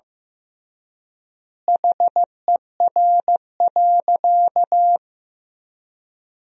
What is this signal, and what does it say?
Morse code, heard as a keyed tone, about 15 wpm, 700 Hz. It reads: HER.
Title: 3x Word-Spacing - Sentences from Top 200 Words - 15wpm